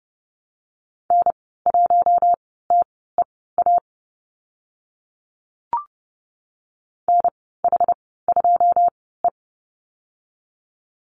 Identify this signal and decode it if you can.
Morse code, heard as a keyed tone, about 30 wpm, spaced as for 17 wpm, 700 Hz. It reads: N1TEA NH2E
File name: Lesson 14g - Number 2 - Call Signs 30wpm